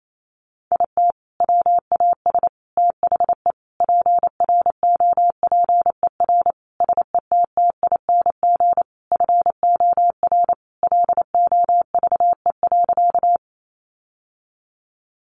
Morse code, heard as a keyed tone, about 28 wpm, 700 Hz. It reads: IT WAS THE PROPER SETTING FOR LOVE.
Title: Sentences from Top 1300 Words 28wpm